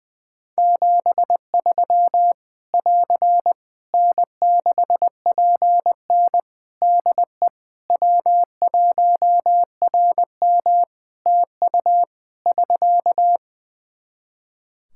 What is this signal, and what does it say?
Morse code, heard as a keyed tone, about 20 wpm, 700 Hz. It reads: 73 <AR> N6PN DE W1RM TU <SK>
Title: Practice QSO - Exchange 4 - Ending - 20wpm